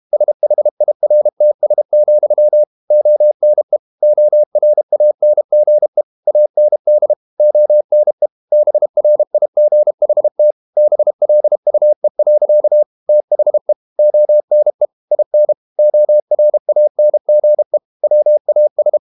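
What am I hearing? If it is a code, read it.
Morse code, heard as a keyed tone, about 32 wpm, 600 Hz. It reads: SHIRTS, ONE ORANGE AND ONE BRIGHT BLUE. THE ONE IN ORANGE WAS
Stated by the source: Morse code practice at 32 wpm from CW med Gnister, reading Ken Follett's novel Never in English.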